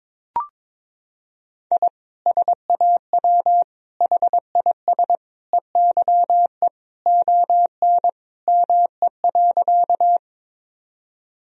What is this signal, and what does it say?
Morse code, heard as a keyed tone, about 22 wpm, 700 Hz. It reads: I SAW HIS EYE ON ME.
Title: Sentences from Top 1000 Words - Review Part 1 - 22wpm